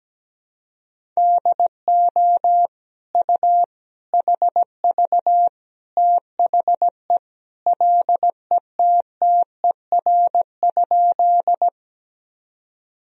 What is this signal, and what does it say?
Morse code, heard as a keyed tone, about 17 wpm, 700 Hz. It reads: DO U HV THE LETTER?
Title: Sentences from Top 300 Words and Abbreviations 17wpm